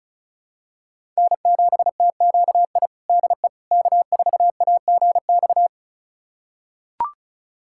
Morse code, heard as a keyed tone, about 36 wpm, 700 Hz. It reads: N7TQI DE K4AGX
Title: USA Callsign DE Callsign 35wpm